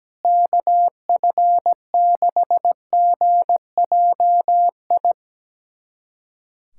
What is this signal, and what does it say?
Morse code, heard as a keyed tone, about 17 wpm, 700 Hz. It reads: KF6GJI
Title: USA Callsign DE Callsign 17wpm